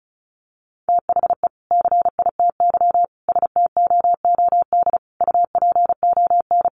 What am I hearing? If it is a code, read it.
Morse code, heard as a keyed tone, about 34 wpm, 700 Hz. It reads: THE CITY STOOD UPON
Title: Sentences from Top 1100 Words 35wpm